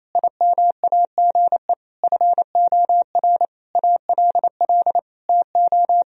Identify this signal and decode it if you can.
Morse code, heard as a keyed tone, about 27 wpm, 700 Hz. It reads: IMAGE FOR ALL TO
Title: Sentences from Top 1000 Words - Review Part 4 - 28wpm